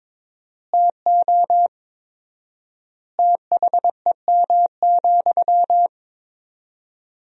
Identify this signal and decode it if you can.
Morse code, heard as a keyed tone, about 22 wpm, 700 Hz. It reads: TO THEM,
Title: 4x Word-Spacing - Sentences from Top 400 Words - 22wpm